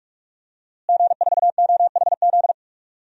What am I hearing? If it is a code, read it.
Morse code, heard as a keyed tone, about 46 wpm, 700 Hz. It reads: G4OHZ
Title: DXCC-100 Call Signs - 5 Characters - Repeated 3x - 45wpm